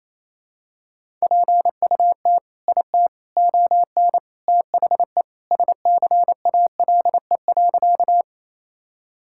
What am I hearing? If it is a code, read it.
Morse code, heard as a keyed tone, about 28 wpm, 700 Hz. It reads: PUT IT ON THE SCALE.
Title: Sentences from Top 1000 Words - Review Part 5 - 28wpm